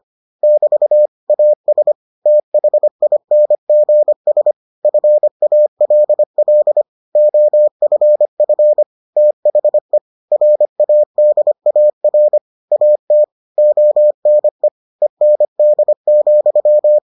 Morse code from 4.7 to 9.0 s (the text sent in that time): FALL OFF